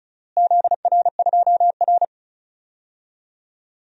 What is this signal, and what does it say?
Morse code, heard as a keyed tone, about 36 wpm, 700 Hz. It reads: ZR2R
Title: DXCC-100 Call Signs - 4 Characters - Repeated 3x - 35wpm